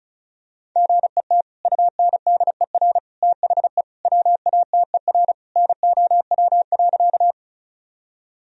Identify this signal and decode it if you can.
Morse code, heard as a keyed tone, about 36 wpm, 700 Hz. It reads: GET UNDER THE WATER NOW.
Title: Sentences from Top 1000 Words - Review Part 1 - 35wpm